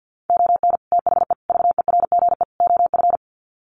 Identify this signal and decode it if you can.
Morse code, heard as a keyed tone, about 50 wpm, 700 Hz. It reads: ON THE VERGE OF
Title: Sets of 4 Words 50wpm